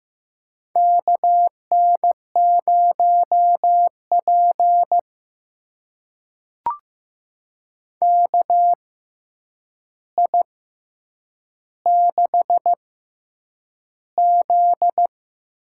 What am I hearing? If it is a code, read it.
Morse code, heard as a keyed tone, about 15 wpm, spaced as for 8 wpm, 700 Hz. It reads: KN0P KI6Z
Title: USA Call Signs - 6x Character Spacing - 15wpm